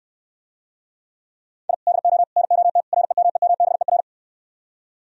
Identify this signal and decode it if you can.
Morse code, heard as a keyed tone, about 70 wpm, 700 Hz. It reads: I DO NOT REMEMBER